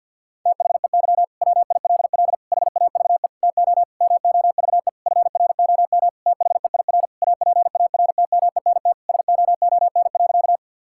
Morse code, heard as a keyed tone, about 50 wpm, 700 Hz. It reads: THEY WILL HAVE TO MOVE FROM THEIR APARTMENT SOON.
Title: Sentences from Top 1000 Words - Review Part 6 - 50wpm